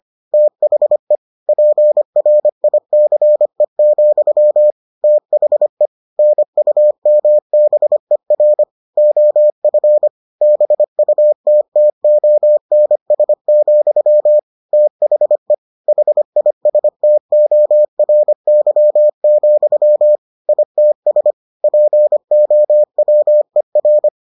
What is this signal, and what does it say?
Morse code, heard as a keyed tone, about 25 wpm, 600 Hz. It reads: THE PRICE, THE NUMBER OF BUTTONS, THE HISTORY, ITS POWER